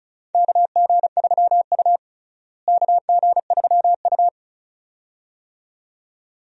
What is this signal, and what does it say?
Morse code, heard as a keyed tone, about 36 wpm, 700 Hz. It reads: KG3U KG3U
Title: USA Call Signs - 4 Characters - Repeated 2x - 35wpm